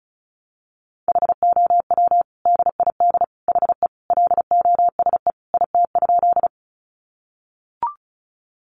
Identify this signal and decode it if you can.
Morse code, heard as a keyed tone, about 34 wpm, 700 Hz. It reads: HOW DID HE LOSE IT?
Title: Sentences from Top 1100 Words 35wpm